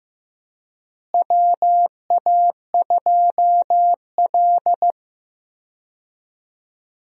Morse code, heard as a keyed tone, about 15 wpm, 700 Hz. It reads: WA2L
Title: USA Call Signs - 4 Characters - Repeated 2x - 15wpm